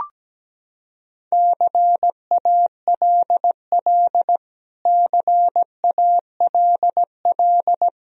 Morse code, heard as a keyed tone, about 17 wpm, 700 Hz. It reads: CALL CALL
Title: Top 100 Words Repeated 17wpm